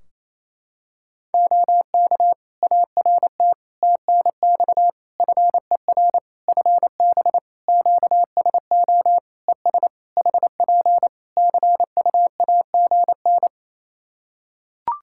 Morse code, heard as a keyed tone, about 27 wpm, 700 Hz. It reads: OK ART TNX FER FB QSO ES HP CUAGN
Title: Practice QSO - Exchange 4 - Ending - 28wpm